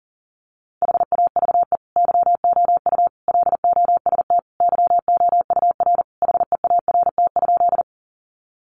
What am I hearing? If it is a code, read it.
Morse code, heard as a keyed tone, about 40 wpm, 700 Hz. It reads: HAVE YOU LOST YOUR HEART?
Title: Sentences from Top 1400 Words 40wpm